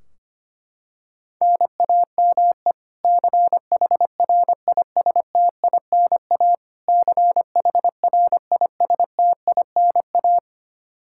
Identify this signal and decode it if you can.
Morse code, heard as a keyed tone, about 25 wpm, 700 Hz. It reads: NAME CHRISTINA CHRISTINA
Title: K1USN SST QSO Practice 25wpm